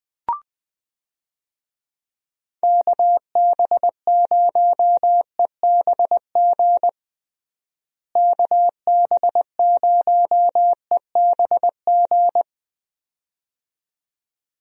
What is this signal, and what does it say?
Morse code, heard as a keyed tone, about 20 wpm, 700 Hz. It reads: KB0EBG KB0EBG
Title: USA Call Signs - 6 Characters - Repeated 2x - 20wpm